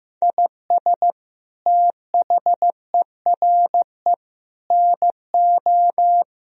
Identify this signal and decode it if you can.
Morse code, heard as a keyed tone, about 15 wpm, 700 Hz. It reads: IS THERE NO
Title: Sentences from Top 1000 Words - Review Part 2 - 15wpm